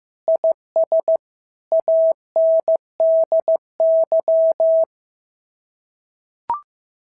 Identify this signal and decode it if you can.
Morse code, heard as a keyed tone, about 14 wpm, 650 Hz. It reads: IS ANDY
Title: SOTA QSO Practice 15wpm